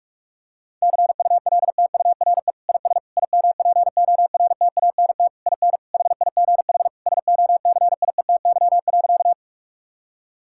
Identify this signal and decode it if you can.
Morse code, heard as a keyed tone, about 41 wpm, 700 Hz. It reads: CULTURE IS IMPORTANT IN HIGH SOCIETY.
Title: Sentences from Top 1000 Words - Review Part 6 - 45wpm